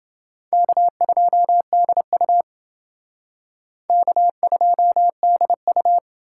Morse code, heard as a keyed tone, about 30 wpm, 700 Hz. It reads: K2DU K2DU